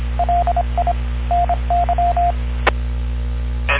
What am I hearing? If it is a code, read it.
Morse code, heard as a keyed tone, about 26 wpm, 700 Hz. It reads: LI NY